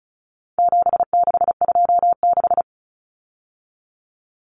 Morse code, heard as a keyed tone, about 34 wpm, 700 Hz. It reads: 7626
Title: Numbers - 4-Digit 35wpm